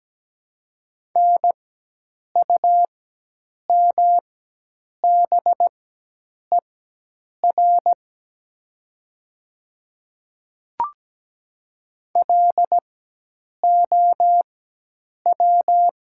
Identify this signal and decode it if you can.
Morse code, heard as a keyed tone, about 17 wpm, spaced as for 8 wpm, 690 Hz. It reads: NUMBER LOW
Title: 4x Character Spacing - Top 300 Words - 17wpm